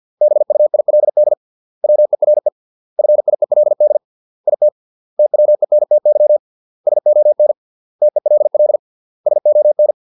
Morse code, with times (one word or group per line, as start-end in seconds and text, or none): 0.2-1.3 BUILD
1.8-2.5 WERE
3.0-4.0 FIELD
4.5-4.7 IT
5.2-6.4 TWENTY
6.9-7.5 SON
8.0-8.8 TELL
9.3-9.9 SON